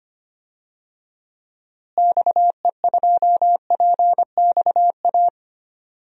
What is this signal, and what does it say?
Morse code, heard as a keyed tone, about 25 wpm, 700 Hz. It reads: XE2PXA
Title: DXCC-100 Call Signs - 6 Characters - Repeated 3x - 25wpm